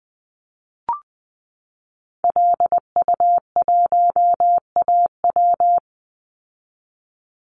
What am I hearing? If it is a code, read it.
Morse code, heard as a keyed tone, about 20 wpm, 700 Hz. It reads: LU1AW
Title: Call Signs Repeated 2x - 20wpm